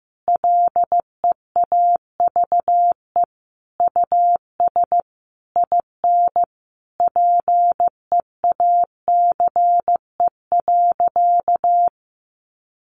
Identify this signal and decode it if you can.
Morse code, heard as a keyed tone, about 15 wpm, 700 Hz. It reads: LEAVE US IN PEACE.
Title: Sentences from Top 1500 Words 15wpm